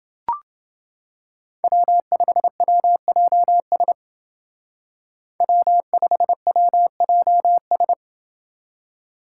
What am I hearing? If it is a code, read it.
Morse code, heard as a keyed tone, about 28 wpm, 700 Hz. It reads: W5WJS W5WJS